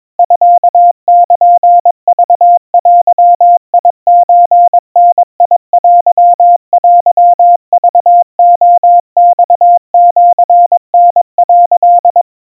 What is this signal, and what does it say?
Morse code (Japanese Wabun code), heard as a keyed tone, about 22 wpm, 700 Hz. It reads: ミルクデソダテテクレマシタ」
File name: kamyu 80 cpm 700 Hz Crispy